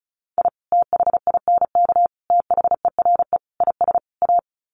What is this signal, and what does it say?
Morse code, heard as a keyed tone, about 34 wpm, 700 Hz. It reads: I THINK THERE IS A
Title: Sets of 5 Words 35wpm